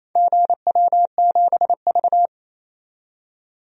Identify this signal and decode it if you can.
Morse code, heard as a keyed tone, about 28 wpm, 700 Hz. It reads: GW7V
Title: DXCC-100 Call Signs - 4 Characters - 28wpm